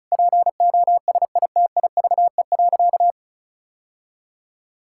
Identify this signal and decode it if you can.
Morse code, heard as a keyed tone, about 36 wpm, 700 Hz. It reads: POSITIVE.
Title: Sentences from Top 1000 Words - Review Part 5 - 35wpm